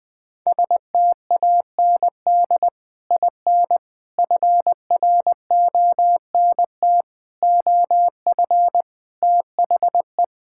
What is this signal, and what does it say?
Morse code, heard as a keyed tone, about 20 wpm, 700 Hz. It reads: STAND IN FRONT OF THE
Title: Sentences with 3-Word Phrases - 20wpm